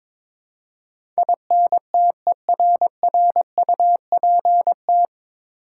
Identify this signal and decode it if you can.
Morse code, heard as a keyed tone, about 22 wpm, 700 Hz. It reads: INTERRUPT
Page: Morse Code Ninja practice 22wpm